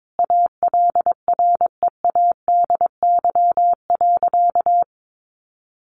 Morse code, heard as a keyed tone, about 22 wpm, 700 Hz. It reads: ALREADY.
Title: Sentences from Top 1000 Words 22wpm